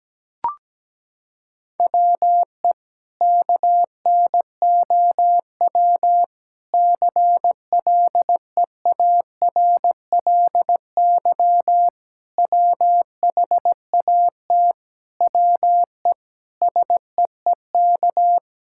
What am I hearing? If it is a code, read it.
Morse code, heard as a keyed tone, about 17 wpm, 700 Hz. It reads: WE KNOW CLEARLY WHAT WE SEEK
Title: Sentences from Top 1600 Words 17wpm